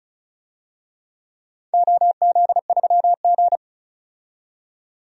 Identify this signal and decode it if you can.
Morse code, heard as a keyed tone, about 36 wpm, 700 Hz. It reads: OZ3G
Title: DXCC-100 Call Signs - 4 Characters - 35wpm